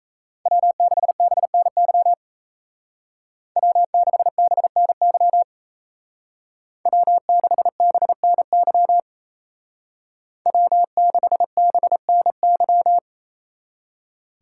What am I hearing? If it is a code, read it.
Morse code, heard as a keyed tone, about 33 wpm, 700 Hz. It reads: W6BNY W6BNY W6BNY W6BNY
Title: Speed Racing - USA Call Signs 28wpm